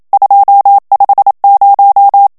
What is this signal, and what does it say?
Morse code, heard as a keyed tone, about 27 wpm, 800 Hz. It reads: 250
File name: CW / FICHIERS-MP3 / FRANCAIS TEXTE CLAIR 1 VITESSE 26 MOTS MINUTES